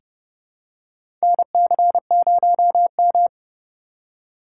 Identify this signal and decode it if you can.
Morse code, heard as a keyed tone, about 31 wpm, 700 Hz. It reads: NC0M
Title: USA Call Signs - 4 Characters - 6x Character Spacing - 30wpm